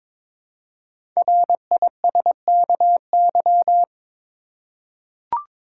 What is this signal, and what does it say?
Morse code, heard as a keyed tone, about 22 wpm, 700 Hz. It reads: RISKY
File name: 5 Letter Words - First 3 Letters Given - 22wpm